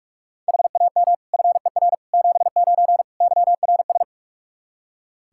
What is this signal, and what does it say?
Morse code, heard as a keyed tone, about 46 wpm, 700 Hz. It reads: HAM FER 79 YRS
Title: Practice QSO - Exchange 3 - More Info - 45wpm